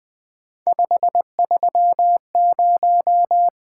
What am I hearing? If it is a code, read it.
Morse code, heard as a keyed tone, about 20 wpm, 700 Hz. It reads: 530
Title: Three Letter-Number Combinations - 20wpm